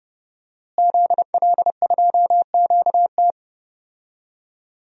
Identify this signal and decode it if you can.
Morse code, heard as a keyed tone, about 31 wpm, 700 Hz. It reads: ZL2QT
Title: DXCC-100 Call Signs - 5 Characters - Repeated 3x - 30wpm